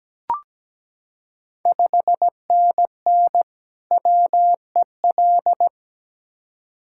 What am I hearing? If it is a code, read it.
Morse code, heard as a keyed tone, about 17 wpm, 700 Hz. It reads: 5NN WEL
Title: Ontario QSO Party Exchange 17wpm